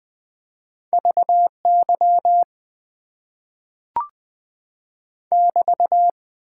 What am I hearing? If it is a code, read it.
Morse code, heard as a keyed tone, about 20 wpm, 700 Hz. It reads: VY <BT>
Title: Top 90 QSO Elements - 6x Character Spacing - 20wpm